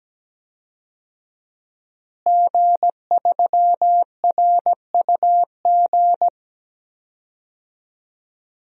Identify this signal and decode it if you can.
Morse code, heard as a keyed tone, about 17 wpm, 700 Hz. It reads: G3RUG
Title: DXCC-100 Call Signs - 5 Characters - Repeated 2x - 17wpm